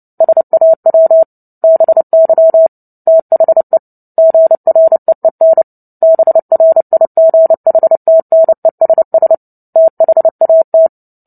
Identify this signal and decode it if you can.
Morse code, heard as a keyed tone, about 30 wpm, 650 Hz. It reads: SAW BY THE GREEN BRIGHTNESS THAT
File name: Book Two / 03-Chapter 3 - 29 WPM